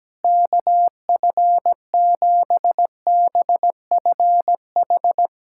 Morse code, heard as a keyed tone, about 17 wpm, 700 Hz. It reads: KF7BFH
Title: USA Callsign DE Callsign 17wpm